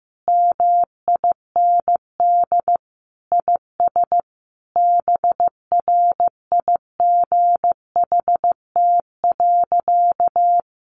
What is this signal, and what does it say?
Morse code, heard as a keyed tone, about 15 wpm, 700 Hz. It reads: MIND IS BRIGHT.
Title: Sentences from Top 1500 Words 15wpm